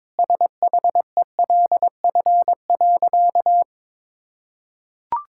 Morse code, heard as a keyed tone, about 22 wpm, 700 Hz. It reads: SHELF.
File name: Sentences with 7-Word Phrases - 22wpm